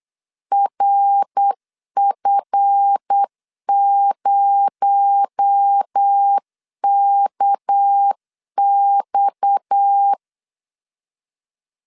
Morse code, heard as a keyed tone, about 8 wpm, 800 Hz. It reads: RF0KX
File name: LEZIONI / 7 GF9Z / 7 GF 8wpm 3108